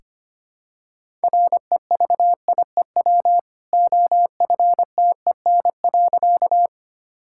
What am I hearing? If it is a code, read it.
Morse code, heard as a keyed tone, about 25 wpm, 700 Hz. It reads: REVIEW OFTEN.
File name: Easy Sentences with Hints - 25wpm